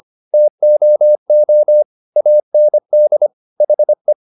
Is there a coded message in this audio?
Yes: Morse code from 0.3 to 4.1 s, heard as a keyed tone, about 25 wpm, 600 Hz. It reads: TOO AND HE